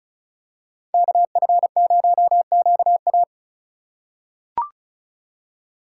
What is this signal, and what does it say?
Morse code, heard as a keyed tone, about 34 wpm, 700 Hz. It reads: KF0QA
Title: USA Call Signs - 5 Characters - 4x Character Spacing - 35wpm